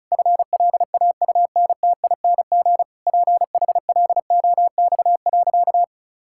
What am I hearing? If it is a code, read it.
Morse code, heard as a keyed tone, about 36 wpm, 700 Hz. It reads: FLAUNTING PHLOX.
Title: Pangram Challenge v2 - 35wpm